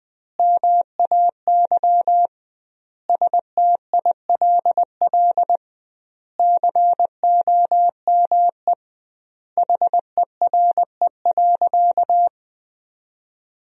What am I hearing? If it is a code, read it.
Morse code, heard as a keyed tone, about 20 wpm, 700 Hz. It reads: MAY STILL COME HERE.
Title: 2x Word-Spacing - Sentences from Top 300 Words - 20wpm